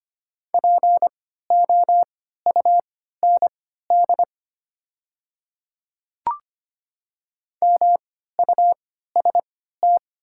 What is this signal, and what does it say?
Morse code, heard as a keyed tone, about 25 wpm, spaced as for 14 wpm, 700 Hz. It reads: POUND MUST
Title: 3x Character Spacing - Top 500 Words - 25wpm